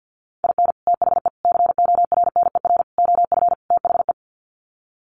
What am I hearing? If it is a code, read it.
Morse code, heard as a keyed tone, about 50 wpm, 700 Hz. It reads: IN THE CORNER OF THE
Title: Sets of 5 Words Encore 50wpm